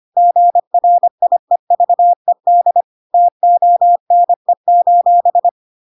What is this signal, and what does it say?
Morse code, heard as a keyed tone, about 25 wpm, 700 Hz. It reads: GRIEVED TONE:
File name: Pinocchio / Pinocchio ch3 25 25 ik1hge